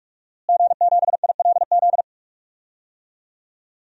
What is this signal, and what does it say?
Morse code, heard as a keyed tone, about 41 wpm, 700 Hz. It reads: G7ILZ
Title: DXCC-100 Call Signs - 5 Characters - Repeated 3x - 45wpm